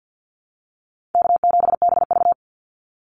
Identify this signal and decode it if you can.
Morse code, heard as a keyed tone, about 50 wpm, 700 Hz. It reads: K7BV
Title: Call Signs - 50wpm